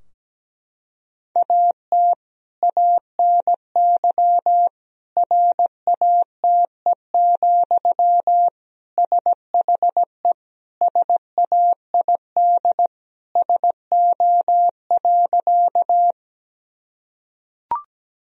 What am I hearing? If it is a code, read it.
Morse code, heard as a keyed tone, about 17 wpm, 700 Hz. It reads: AT ANY RATE, SHE SAID SO.
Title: Sentences from Top 1000 Words - Review Part 4 - 17wpm